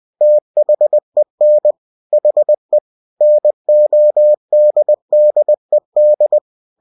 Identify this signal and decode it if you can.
Morse code, heard as a keyed tone, about 20 wpm, 600 Hz. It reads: THEN HE NODDED